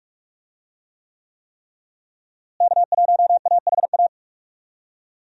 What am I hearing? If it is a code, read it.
Morse code, heard as a keyed tone, about 46 wpm, 700 Hz. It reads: K1AHA
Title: USA Call Signs - 5 Characters - Repeated 3x - 45wpm